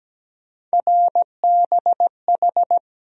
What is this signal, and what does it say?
Morse code, heard as a keyed tone, about 17 wpm, 700 Hz. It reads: RBH
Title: USA Call Sign Postfixes - 3 Letters 17wpm